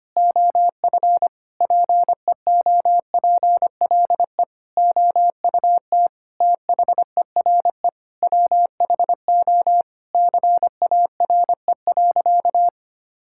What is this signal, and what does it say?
Morse code, heard as a keyed tone, about 25 wpm, 700 Hz. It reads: OF PEOPLE OUT THERE WHO CARE.
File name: practice / 25wpm / Sentences with 9-Word Phrases - 25wpm